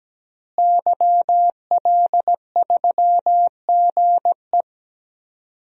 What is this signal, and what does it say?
Morse code, heard as a keyed tone, about 17 wpm, 700 Hz. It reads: YL3GE
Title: DXCC-100 Call Signs - 5 Characters - Repeated 3x - 17wpm